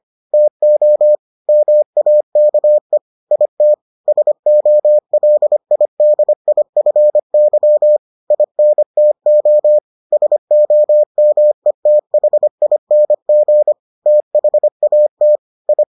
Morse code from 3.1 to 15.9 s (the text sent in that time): IT SOLIDIFY INTO SOMETHING THAT I